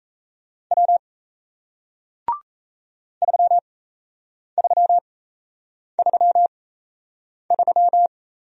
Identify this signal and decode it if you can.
Morse code, heard as a keyed tone, about 38 wpm, spaced as for 8 wpm, 700 Hz. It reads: W 3333